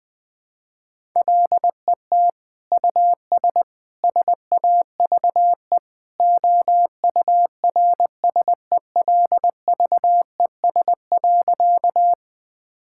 Morse code, heard as a keyed tone, about 20 wpm, 700 Hz. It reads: LET US SAVE OURSELVES.
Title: Sentences from Top 1600 Words 20wpm